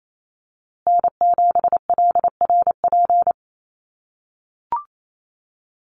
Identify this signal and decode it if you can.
Morse code, heard as a keyed tone, about 28 wpm, 700 Hz. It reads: N7LRP
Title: USA Call Signs Repeated 2x 28wpm